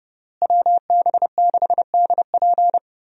Morse code, heard as a keyed tone, about 31 wpm, 700 Hz. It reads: WB6DP
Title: USA Call Signs - 5 Characters - Repeated 3x - 30wpm